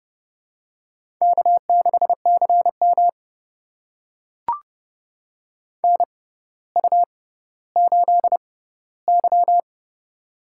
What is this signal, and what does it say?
Morse code, heard as a keyed tone, about 30 wpm, spaced as for 15 wpm, 700 Hz. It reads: K6CM NU8Y
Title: USA Call Signs - 4 Characters - 6x Character Spacing - 30wpm